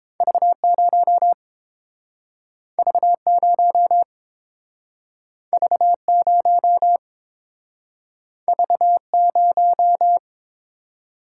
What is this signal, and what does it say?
Morse code, heard as a keyed tone, about 29 wpm, 700 Hz. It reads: V0 V0 V0 V0